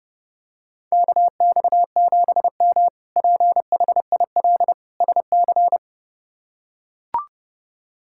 Morse code, heard as a keyed tone, about 30 wpm, 700 Hz. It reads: KX7M PHIL SC
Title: K1USN SST QSO Practice - 30wpm